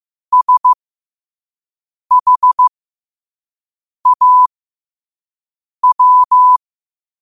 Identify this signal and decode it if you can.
Morse code, heard as a keyed tone, about 15 wpm, spaced as for 5 wpm, 1000 Hz. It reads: SHAW
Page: Quotes for Fri, 15 Aug 2025 in Morse Code at 5 words per minute.